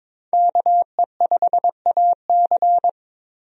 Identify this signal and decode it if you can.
Morse code, heard as a keyed tone, about 22 wpm, 700 Hz. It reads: KE5AC
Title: USA Call Signs - 5 Characters - Repeated 2x - 22wpm